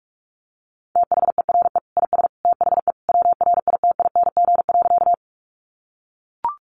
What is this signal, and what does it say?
Morse code, heard as a keyed tone, about 44 wpm, 700 Hz. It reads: THERE IS THE WRITING.
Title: Sentences from Top 800 Words 45wpm